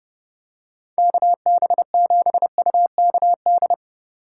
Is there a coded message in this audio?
Yes: Morse code from 1.0 to 3.7 s, heard as a keyed tone, about 29 wpm, 700 Hz. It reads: KB7UKD